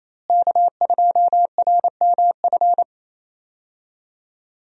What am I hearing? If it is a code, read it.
Morse code, heard as a keyed tone, about 28 wpm, 700 Hz. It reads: K2RMF
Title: USA Call Signs - 5 Characters - Repeated 2x - 28wpm